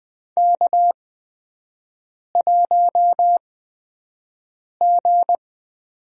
Morse code, heard as a keyed tone, about 20 wpm, spaced as for 5 wpm, 700 Hz. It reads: K1G
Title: USA Call Signs - 5 Characters - 8x Character Spacing - 20wpm